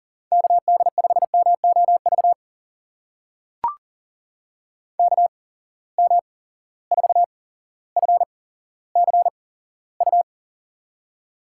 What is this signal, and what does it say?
Morse code, heard as a keyed tone, about 40 wpm, spaced as for 17 wpm, 700 Hz. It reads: KD5MOV KM4FCU
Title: USA Call Signs - 6 Characters - 8x Character Spacing - 40wpm